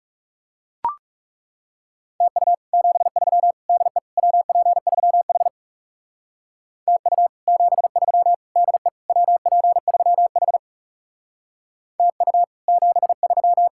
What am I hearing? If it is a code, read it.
Morse code, heard as a keyed tone, about 36 wpm, 700 Hz. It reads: TU 73 DE WP3H TU 73 DE WP3H TU 73
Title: Speed-Racing Common QSO Phrases Encore - 30wpm